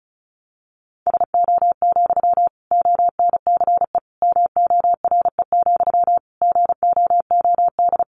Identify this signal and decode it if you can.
Morse code, heard as a keyed tone, about 34 wpm, 700 Hz. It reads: SO, ONCE MORE, GOOD
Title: Sentences from Top 300 Words 35wpm